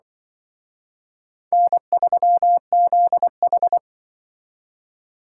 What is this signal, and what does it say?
Morse code, heard as a keyed tone, about 24 wpm, 700 Hz. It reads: N3ZH
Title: Speed Racing - USA Call Signs 20wpm